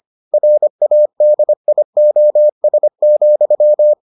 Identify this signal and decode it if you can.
Morse code, heard as a keyed tone, about 25 wpm, 600 Hz. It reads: RADIOS,